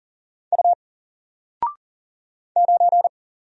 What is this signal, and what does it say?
Morse code, heard as a keyed tone, about 39 wpm, 700 Hz. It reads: U 9
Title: Single Character (Letter:Number:Punctuation:Prosign) Alt 40wpm